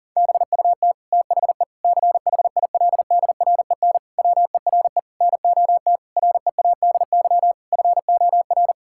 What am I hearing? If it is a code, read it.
Morse code, heard as a keyed tone, about 40 wpm, 700 Hz. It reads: BUT THE CHILDREN WERE NOT READY FOR
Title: Sentences from Top 1000 Words - Review Part 6 - 40wpm